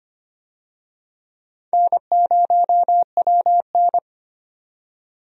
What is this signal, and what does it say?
Morse code, heard as a keyed tone, about 25 wpm, 700 Hz. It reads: N0WN
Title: USA Call Signs - 4 Characters - 25wpm